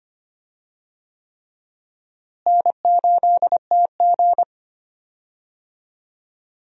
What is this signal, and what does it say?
Morse code, heard as a keyed tone, about 25 wpm, 700 Hz. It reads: N8TG
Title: USA Call Signs - 4 Characters - Repeated 3x - 25wpm